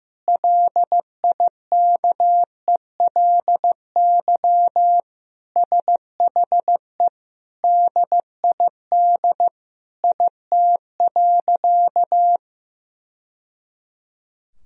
Morse code, heard as a keyed tone, about 15 wpm, 700 Hz. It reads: LIKELY SHE DID IT.
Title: Sentences from Top 1000 Words - Review Part 4 - 15wpm